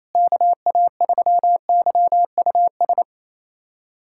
Morse code, heard as a keyed tone, about 27 wpm, 700 Hz. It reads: KA3YUS
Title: USA Callsign DE Callsign - 6 Characters - 28wpm